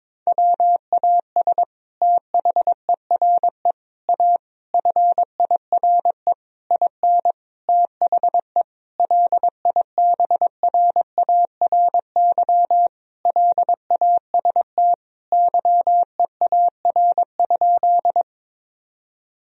Morse code, heard as a keyed tone, about 22 wpm, 700 Hz. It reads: WAS THERE A FIRE IN THE LIBRARY LAST YEAR?
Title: Sentences from Top 1000 Words - Review Part 4 - 22wpm